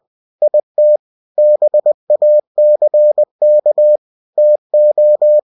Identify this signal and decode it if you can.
Morse code, heard as a keyed tone, about 20 wpm, 600 Hz. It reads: IT BACK TO
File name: cw-books / winnie-the-pooh / winnie-20wpm-page-0032